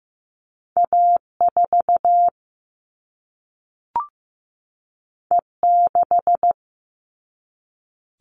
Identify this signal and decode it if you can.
Morse code, heard as a keyed tone, about 15 wpm, 700 Hz. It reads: A4 E6
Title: All Cut Numbers - 2-Digit 15wpm